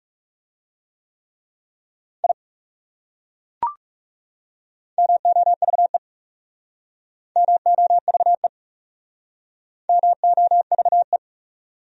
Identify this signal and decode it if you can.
Morse code, heard as a keyed tone, about 38 wpm, 700 Hz. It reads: I MOVE MOVE MOVE